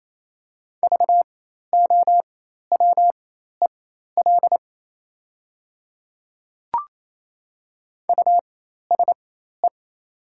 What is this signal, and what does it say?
Morse code, heard as a keyed tone, about 28 wpm, spaced as for 13 wpm, 700 Hz. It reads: VOWEL USE
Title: 4x Character Spacing - Top 500 Words - 28wpm